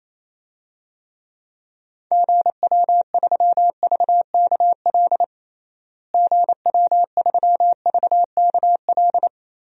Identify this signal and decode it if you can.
Morse code, heard as a keyed tone, about 28 wpm, 700 Hz. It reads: GW3VKL GW3VKL
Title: DXCC-100 Call Signs - 6 Characters - Repeated 3x - 28wpm